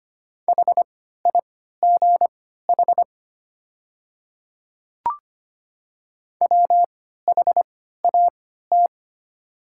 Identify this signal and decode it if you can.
Morse code, heard as a keyed tone, about 25 wpm, spaced as for 14 wpm, 700 Hz. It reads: HIGH WHAT
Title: 3x Character Spacing - Top 200 Words - 25wpm